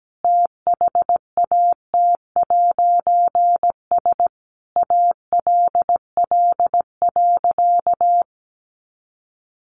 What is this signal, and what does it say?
Morse code, heard as a keyed tone, about 17 wpm, 700 Hz. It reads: THAT'S ALL.
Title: A Sound Of Thunder 17wpm